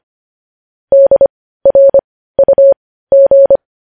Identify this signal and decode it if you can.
Morse code, heard as a keyed tone, about 25 wpm, spaced as for 15 wpm, 550 Hz. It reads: DRUG
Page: news-headlines-in-morse-code-at-15-wpm